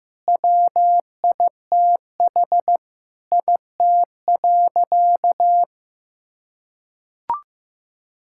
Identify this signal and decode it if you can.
Morse code, heard as a keyed tone, about 15 wpm, 700 Hz. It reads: WITH IT.
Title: Sentences from Top 1000 Words - Review Part 6 - 15wpm